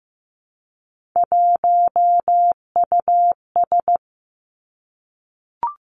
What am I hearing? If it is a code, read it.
Morse code, heard as a keyed tone, about 15 wpm, 700 Hz. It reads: 1US
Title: Three Challenging Letter-Number Combinations 15wpm